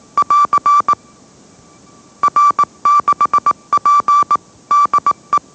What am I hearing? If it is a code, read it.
Morse code, heard as a keyed tone, about 18 wpm, 1160 Hz. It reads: <AR> R6P DE
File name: mrx / mrx081 20wpm